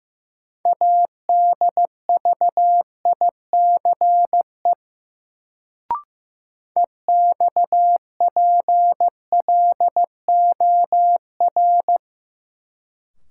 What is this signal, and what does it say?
Morse code, heard as a keyed tone, about 15 wpm, 700 Hz. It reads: ADVICE EXPLOR
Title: Sentences from Top 1100 Words - New Word Drill - 15wpm